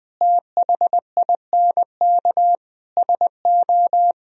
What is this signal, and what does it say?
Morse code, heard as a keyed tone, about 20 wpm, 700 Hz. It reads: THINK SO
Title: Speed-Racing - Sets of 5 Words 17wpm